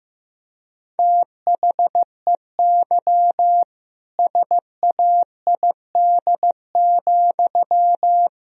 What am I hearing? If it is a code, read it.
Morse code, heard as a keyed tone, about 15 wpm, 700 Hz. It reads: THEY SAID,
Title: Sentences from Top 1000 Words - Review Part 1 - 15wpm